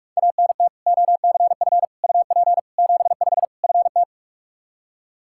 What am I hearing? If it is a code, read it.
Morse code, heard as a keyed tone, about 45 wpm, 700 Hz. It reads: ANT OCF UP 75 FT
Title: Practice QSO - Exchange 2 - Info - 45wpm